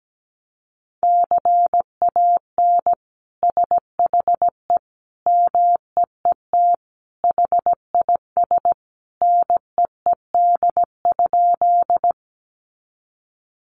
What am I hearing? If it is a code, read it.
Morse code, heard as a keyed tone, about 17 wpm, 700 Hz. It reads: CAN SHE MEET HIS NEED?
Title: Sentences from Top 1100 Words 17wpm